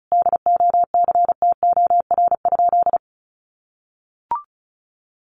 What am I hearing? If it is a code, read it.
Morse code, heard as a keyed tone, about 34 wpm, 700 Hz. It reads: DOCTOR?
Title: Sentences from Top 1500 Words 35wpm